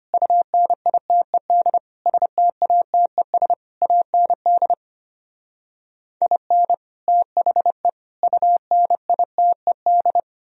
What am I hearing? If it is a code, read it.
Morse code, heard as a keyed tone, about 33 wpm, 700 Hz. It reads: UNITED STATES AND IN THE UNITED